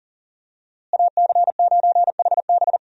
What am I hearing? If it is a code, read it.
Morse code, heard as a keyed tone, about 37 wpm, 700 Hz. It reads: AC9HB
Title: USA Call Signs - 5 Characters - Repeated 3x - 40wpm